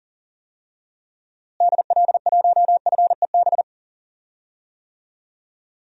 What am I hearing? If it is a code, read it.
Morse code, heard as a keyed tone, about 39 wpm, 700 Hz. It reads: DL1FEB
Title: DXCC-100 Call Signs - 6 Characters - Repeated 3x - 40wpm